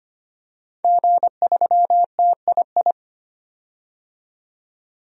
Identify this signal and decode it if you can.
Morse code, heard as a keyed tone, about 25 wpm, 700 Hz. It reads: G3TII
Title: DXCC-100 Call Signs - 5 Characters - 25wpm